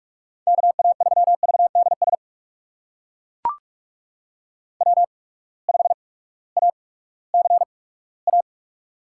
Morse code, heard as a keyed tone, about 42 wpm, spaced as for 20 wpm, 700 Hz. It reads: KA3VDS W5ACA
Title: USA Call Signs - 8x Character Spacing - 45wpm